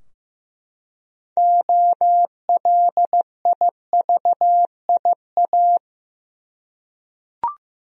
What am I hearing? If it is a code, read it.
Morse code, heard as a keyed tone, about 15 wpm, 700 Hz. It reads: OLIVIA
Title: USA Common Names - 15wpm